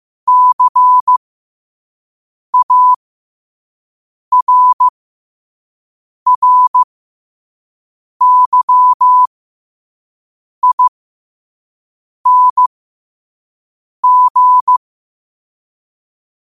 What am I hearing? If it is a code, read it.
Morse code, heard as a keyed tone, about 15 wpm, spaced as for 5 wpm, 1000 Hz. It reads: CARRYING
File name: qotdfiles / 5wpm